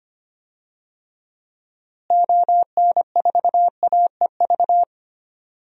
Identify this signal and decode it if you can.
Morse code, heard as a keyed tone, about 25 wpm, 700 Hz. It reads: ON4AEV
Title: DXCC-100 Call Signs - 6 Characters - Repeated 3x - 25wpm